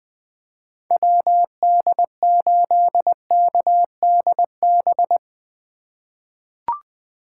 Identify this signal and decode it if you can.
Morse code, heard as a keyed tone, about 20 wpm, 700 Hz. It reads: WD8KDB
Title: 2023 SOTA Chaser Call Signs - 20wpm